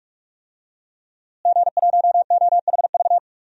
Morse code, heard as a keyed tone, about 46 wpm, 700 Hz. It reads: G1OHV
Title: DXCC-100 Call Signs - 5 Characters - Repeated 2x - 45wpm